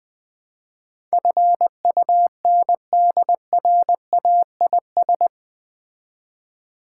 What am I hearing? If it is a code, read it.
Morse code, heard as a keyed tone, about 20 wpm, 700 Hz. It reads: FUNDRAIS
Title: Build A Word - 4 Letters At A Time - 11-Letter Words - 20wpm